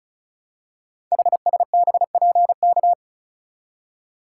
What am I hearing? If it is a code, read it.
Morse code, heard as a keyed tone, about 36 wpm, 700 Hz. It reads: HSBPK
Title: Code Groups - 5 Letters - 35wpm